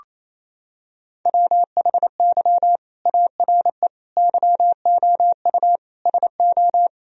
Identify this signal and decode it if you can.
Morse code, heard as a keyed tone, about 28 wpm, 700 Hz. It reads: WHY ARE YOU SO
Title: Sentences from Top 1000 Words - Review Part 2 - 28wpm